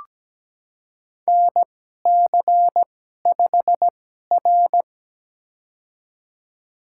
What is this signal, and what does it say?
Morse code, heard as a keyed tone, about 17 wpm, spaced as for 12 wpm, 700 Hz. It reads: NC5R
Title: USA Call Signs - 4 Characters - 2x Character Spacing - 17wpm